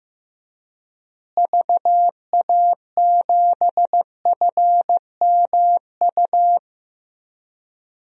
Morse code, heard as a keyed tone, about 15 wpm, 700 Hz. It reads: VA7FMU
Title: DXCC-100 Call Signs - 6 Characters - Repeated 2x - 15wpm